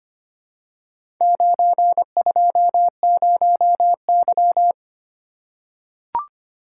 Morse code, heard as a keyed tone, about 26 wpm, 700 Hz. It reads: 920Y
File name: practice / 25wpm / Four Letter-Number Combinations - 25wpm